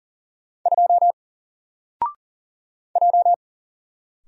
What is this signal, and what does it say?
Morse code, heard as a keyed tone, about 40 wpm, 700 Hz. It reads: 2 J
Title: Single Letter-Number Alt 40wpm